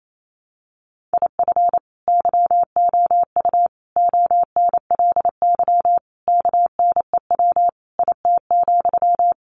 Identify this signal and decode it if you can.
Morse code, heard as a keyed tone, about 28 wpm, 700 Hz. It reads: IF YOU ONLY KNEW IT,
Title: Sentences from Top 1500 Words 28wpm